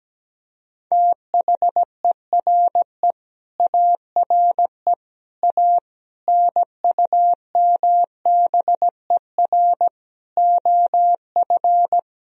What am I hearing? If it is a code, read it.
Morse code, heard as a keyed tone, about 17 wpm, 700 Hz. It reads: THERE ARE A NUMBER OF
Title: Speed-Racing - Sets of 5 Words 17wpm